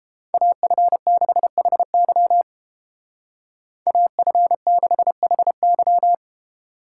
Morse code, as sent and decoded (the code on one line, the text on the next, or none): .- ..-. -.... .... -.-- / .- ..-. -.... .... -.--
AF6HY AF6HY